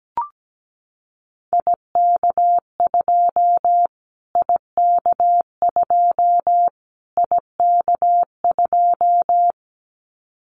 Morse code, heard as a keyed tone, about 17 wpm, 700 Hz. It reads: IK2 IK2 IK2
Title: Callsign Prefixes for Top 25 DXCC - Repeated 3x - 17wpm